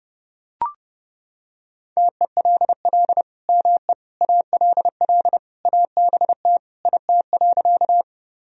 Morse code, heard as a keyed tone, about 30 wpm, 700 Hz. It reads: TELL ME ALL ABT IT.
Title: Sentences from Top 200 Words and Abbreviations 30wpm